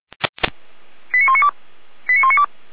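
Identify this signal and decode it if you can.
Five-tone selective call (ZVEI), heard as quick runs of five tones, tones of about 70 ms. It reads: 89192 89192
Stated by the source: ZVEI five-tone selective call 89333 received by SDR.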